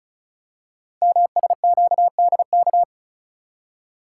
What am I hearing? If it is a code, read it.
Morse code, heard as a keyed tone, about 36 wpm, 690 Hz. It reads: MSQDK
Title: Code Groups - 5 Letters - 35wpm